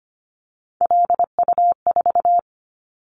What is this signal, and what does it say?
Morse code, heard as a keyed tone, about 25 wpm, 700 Hz. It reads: LU4
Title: Callsign Prefixes for Top 100 DXCC - 25wpm